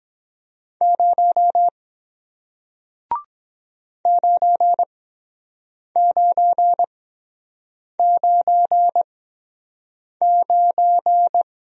Morse code, heard as a keyed tone, about 23 wpm, spaced as for 7 wpm, 700 Hz. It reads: 0 9999